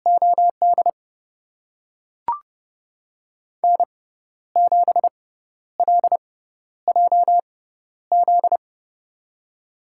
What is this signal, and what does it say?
Morse code, heard as a keyed tone, about 31 wpm, spaced as for 12 wpm, 700 Hz. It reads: OD N7LJZ
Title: USA Call Signs - 5 Characters - 6x Character Spacing - 30wpm